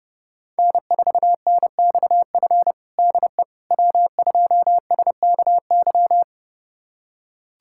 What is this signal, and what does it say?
Morse code, heard as a keyed tone, about 32 wpm, 700 Hz. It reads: N4NXF DE W2SKY
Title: USA Callsign DE Callsign - 5 Characters - 30wpm